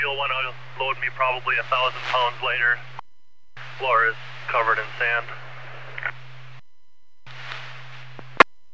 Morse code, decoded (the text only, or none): none